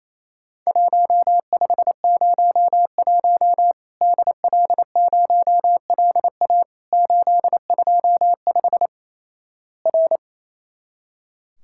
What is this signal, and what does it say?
Morse code, heard as a keyed tone, about 28 wpm, 700 Hz. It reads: 1501 DL0LA 825 R
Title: Worked All Europe DX Contest - QTC Exchange - 28wpm